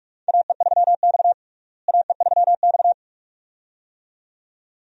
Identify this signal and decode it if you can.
Morse code, heard as a keyed tone, about 42 wpm, 700 Hz. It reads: AE3X AE3X
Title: USA Call Signs - 4 Characters - Repeated 2x - 45wpm